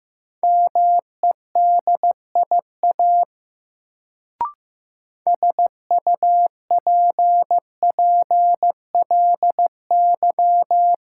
Morse code, heard as a keyed tone, about 15 wpm, 700 Hz. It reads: MEDIA SUPPLY
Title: Sentences from Top 1000 Words - Review Part 4 - New Word Drill - 15wpm